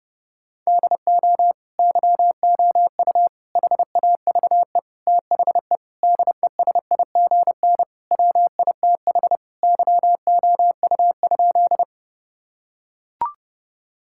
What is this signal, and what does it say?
Morse code, heard as a keyed tone, about 30 wpm, 700 Hz. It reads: DO YOU HAVE THE DESIGN WITH YOU?
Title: Sentences from Top 1000 Words - Review Part 5 - 30wpm